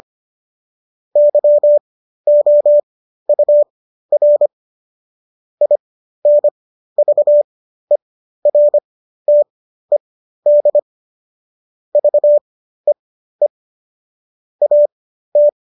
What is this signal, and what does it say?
Morse code, heard as a keyed tone, about 25 wpm, spaced as for 13 wpm, 590 Hz. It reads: YOUR INVERTED VEE AT